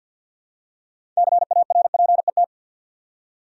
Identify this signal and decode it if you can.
Morse code, heard as a keyed tone, about 46 wpm, 700 Hz. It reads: CARPET
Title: Six Letter Words - 50wpm